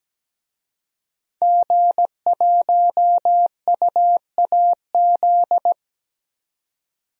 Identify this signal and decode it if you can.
Morse code, heard as a keyed tone, about 17 wpm, 700 Hz. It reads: G1UAZ
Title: DXCC-100 Call Signs - 5 Characters - Repeated 2x - 17wpm